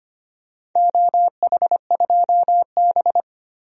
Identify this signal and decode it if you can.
Morse code, heard as a keyed tone, about 25 wpm, 700 Hz. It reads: OH2B
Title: DXCC-100 Call Signs - 4 Characters - Repeated 2x - 25wpm